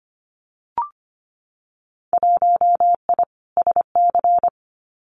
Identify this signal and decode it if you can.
Morse code, heard as a keyed tone, about 25 wpm, 700 Hz. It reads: 1I SC
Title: Winter Field Day Exchanges Encore 25wpm